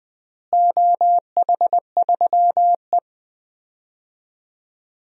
Morse code, heard as a keyed tone, about 20 wpm, 700 Hz. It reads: OH3E
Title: DXCC-100 Call Signs - 4 Characters - Repeated 3x - 20wpm